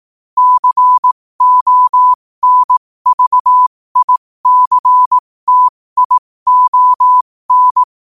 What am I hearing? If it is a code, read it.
Morse code, heard as a keyed tone, about 18 wpm, spaced as for 15 wpm, 1000 Hz. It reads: CONVICTION
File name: qotdfiles / 15wpm